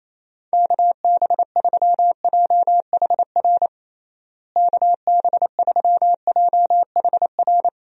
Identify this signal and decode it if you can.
Morse code, heard as a keyed tone, about 28 wpm, 700 Hz. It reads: KB3JHR KB3JHR
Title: USA Call Signs - 6 Characters - Repeated 2x - 28wpm